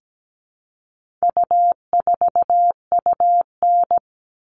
Morse code, heard as a keyed tone, about 17 wpm, 700 Hz. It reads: U4UN
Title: All Cut Numbers - 4-Digit 17wpm